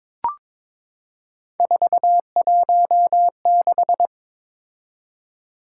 Morse code, heard as a keyed tone, about 22 wpm, 700 Hz. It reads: 416
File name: practice / 22wpm / Three Letter-Number Combinations - 22wpm